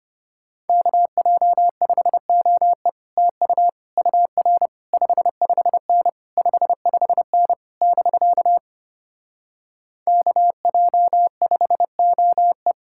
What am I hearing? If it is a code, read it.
Morse code, heard as a keyed tone, about 33 wpm, 700 Hz. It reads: KJ5OE TU UR 55N 55N <BK> KJ5OE